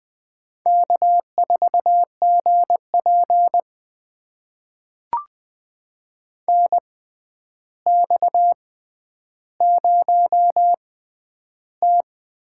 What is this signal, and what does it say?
Morse code, heard as a keyed tone, about 20 wpm, spaced as for 10 wpm, 700 Hz. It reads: K4GP NX0T
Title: USA Call Signs - 4 Characters - 6x Character Spacing - 20wpm